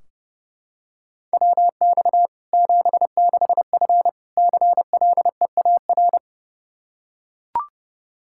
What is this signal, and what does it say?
Morse code, heard as a keyed tone, about 30 wpm, 700 Hz. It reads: WX 76F CLEAR
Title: K1USN SST QSO Practice 30wpm